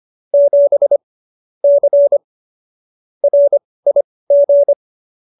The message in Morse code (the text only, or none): 7 C RIG